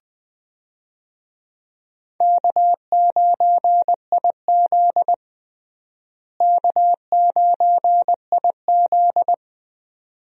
Morse code, heard as a keyed tone, about 20 wpm, 700 Hz. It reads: K9IZ K9IZ
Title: USA Call Signs - 4 Characters - Repeated 2x - 20wpm